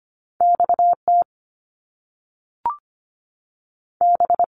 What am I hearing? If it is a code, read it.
Morse code, heard as a keyed tone, about 25 wpm, 700 Hz. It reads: XT B
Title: Top Two Letter Combinations in Words 25wpm